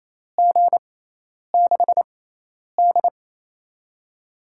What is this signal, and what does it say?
Morse code, heard as a keyed tone, about 28 wpm, spaced as for 10 wpm, 700 Hz. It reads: G6D
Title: USA Call Signs - 6x Character Spacing - 28wpm